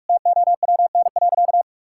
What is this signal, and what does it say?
Morse code, heard as a keyed tone, about 46 wpm, 700 Hz. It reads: TOWN.
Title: Sentences from Top 1000 Words - Review Part 6 - 45wpm